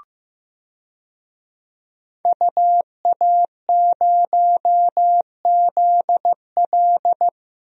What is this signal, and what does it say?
Morse code, heard as a keyed tone, about 15 wpm, 700 Hz. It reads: UA0ZL
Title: DXCC-100 Call Signs - 5 Characters - Repeated 3x - 15wpm